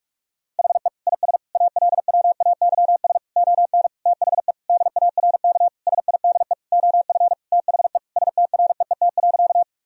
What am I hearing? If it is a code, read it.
Morse code, heard as a keyed tone, about 46 wpm, 700 Hz. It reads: HE IS ALWAYS ON THE DARK SIDE OF THE STREET.
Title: Sentences from Top 1000 Words - Review Part 3 - 45wpm